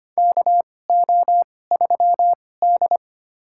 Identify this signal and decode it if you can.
Morse code, heard as a keyed tone, about 25 wpm, spaced as for 18 wpm, 700 Hz. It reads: KO3D
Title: USA Call Signs - 4 Characters - 2x Character Spacing - 25wpm